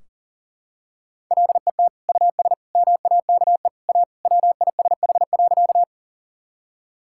Morse code, heard as a keyed tone, about 37 wpm, 700 Hz. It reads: LET US MAKE A WISH.
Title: Sentences from Top 1600 Words 40wpm